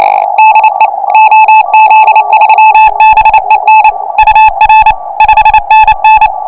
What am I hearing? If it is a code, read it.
Morse code, heard as a keyed tone, about 28 wpm, 2600 Hz, with noise. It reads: DE OZ3BEN UR 5NN
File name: attachments / pileup 1